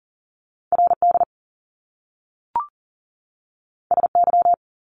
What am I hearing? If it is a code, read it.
Morse code, heard as a keyed tone, about 40 wpm, 700 Hz. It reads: RD SY
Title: Top Two Letter Combinations in Words 40wpm